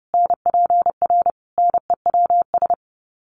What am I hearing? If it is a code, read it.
Morse code, heard as a keyed tone, about 30 wpm, 700 Hz. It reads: NPR NEWS
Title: Sets of 5 Words Encore 30wpm